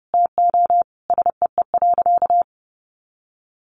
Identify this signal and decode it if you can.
Morse code, heard as a keyed tone, about 30 wpm, 700 Hz. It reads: TO SEE.
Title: Sentences from Top 200 Words 30wpm